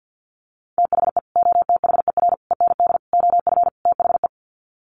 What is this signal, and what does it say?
Morse code, heard as a keyed tone, about 50 wpm, 700 Hz. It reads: THE OTHER END OF THE
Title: Sets of 5 Words 50wpm